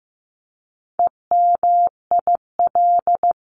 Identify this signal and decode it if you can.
Morse code, heard as a keyed tone, about 15 wpm, 700 Hz. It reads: EMIL
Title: CWT Exchange 15wpm